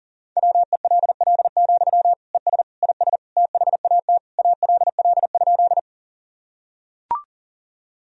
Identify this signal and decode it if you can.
Morse code, heard as a keyed tone, about 37 wpm, 700 Hz. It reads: WELL, ES IS THAT ALL?
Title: Sentences from Top 200 Words and Abbreviations 40wpm